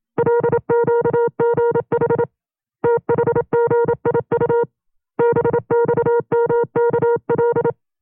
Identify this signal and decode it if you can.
Morse code, heard as a keyed tone, about 27 wpm, 470 Hz. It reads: LQGH THGIU BXMKL
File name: 25WPM